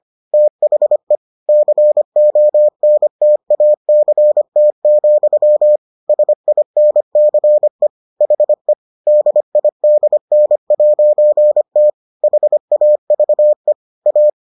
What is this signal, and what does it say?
Morse code, heard as a keyed tone, about 25 wpm, 600 Hz. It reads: THE CONTACT, SINCE HE DIDN'T HAVE A